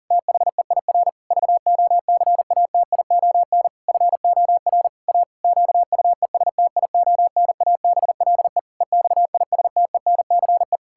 Morse code, heard as a keyed tone, about 40 wpm, 700 Hz. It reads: THEIR VOCATION FOR A QUESTIONABLE EXISTENCE
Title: Pangram Challenge v2 - 40wpm